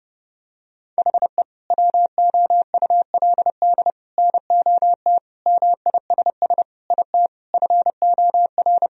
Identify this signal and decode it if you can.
Morse code, heard as a keyed tone, about 30 wpm, 700 Hz. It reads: HE WOULD NOT MISS IT FOR
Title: Sentences from Top 1000 Words - Review Part 3 - 30wpm